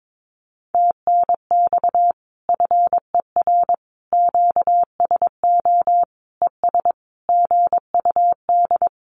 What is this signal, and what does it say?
Morse code, heard as a keyed tone, about 22 wpm, 700 Hz. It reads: TNX FER QSO ES GUD
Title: Common QSO Phrases Encore 22wpm